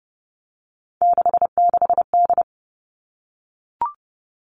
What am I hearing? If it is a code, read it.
Morse code, heard as a keyed tone, about 30 wpm, 700 Hz. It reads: 66D
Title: Three Challenging Letter-Number Combinations 30wpm